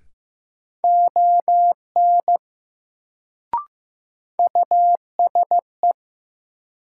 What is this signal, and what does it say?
Morse code, heard as a keyed tone, about 15 wpm, 700 Hz. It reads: ON USE
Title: Sets of 2 Words - New Word Drill - 15wpm